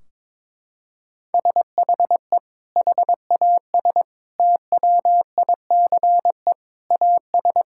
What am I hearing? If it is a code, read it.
Morse code, heard as a keyed tone, about 22 wpm, 700 Hz. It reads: SHE HAS TWICE AS
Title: Sentences from Top 1600 Words 22wpm